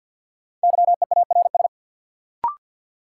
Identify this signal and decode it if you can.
Morse code, heard as a keyed tone, about 53 wpm, 700 Hz. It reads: YEARS
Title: Sentences from Top 200 Words - New Word Drill - 50wpm